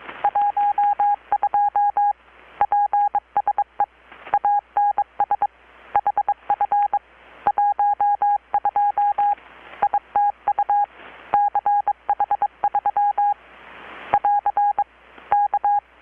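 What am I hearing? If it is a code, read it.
Morse code, heard as a keyed tone, about 22 wpm, 820 Hz. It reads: 12 PSE ANS HF 12 ITU CH3 <AR> K
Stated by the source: Morse from KPH's Night of Nights XX maritime radiotelegraphy event, received in Chiapas, Mexico about 3,800 km away on a Kenwood R-600 with a grounded 10 m long-wire antenna.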